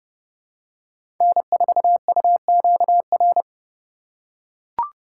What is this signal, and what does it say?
Morse code, heard as a keyed tone, about 30 wpm, 700 Hz. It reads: N4UQR
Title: USA Call Signs - 3x Character Spacing - 30wpm